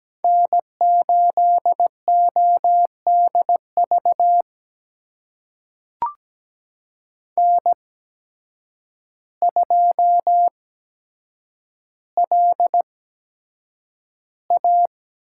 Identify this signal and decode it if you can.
Morse code, heard as a keyed tone, about 17 wpm, spaced as for 8 wpm, 700 Hz. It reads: N8ODV N2LA
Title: USA Call Signs - 8x Character Spacing - 17wpm